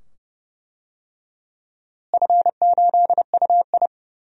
Morse code, heard as a keyed tone, about 30 wpm, 700 Hz. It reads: F8UI
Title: DXCC-100 Call Signs - 4 Characters - Repeated 2x - 30wpm